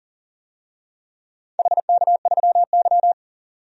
Morse code, heard as a keyed tone, about 40 wpm, 700 Hz. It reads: HK3Y
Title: DXCC-100 Call Signs - 4 Characters - 40wpm